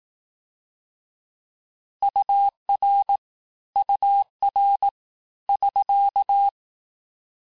Morse code, heard as a keyed tone, about 18 wpm, 800 Hz. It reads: UR UR <SK>
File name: cours-du-lundi files / cours-1852-abreviations-vitesse-17MPM-puis-22MPM